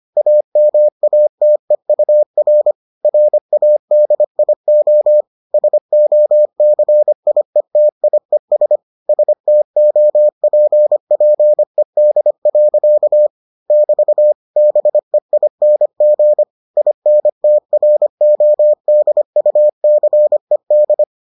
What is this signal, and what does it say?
Morse code, heard as a keyed tone, about 25 wpm, 600 Hz. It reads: AMATEUR RADIO SOCIETIES STOPPED. <BT> BEING INTRODUCED